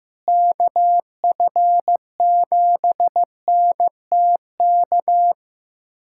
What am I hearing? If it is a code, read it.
Morse code, heard as a keyed tone, about 16 wpm, 700 Hz. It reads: KF7NTK
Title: USA Callsign DE Callsign - 6 Characters - 15wpm